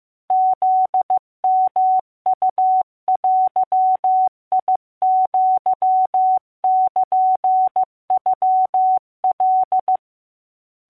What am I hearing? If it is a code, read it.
Morse code (Japanese Wabun code), heard as a keyed tone, about 15 wpm, 750 Hz. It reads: フヨウデアルノカ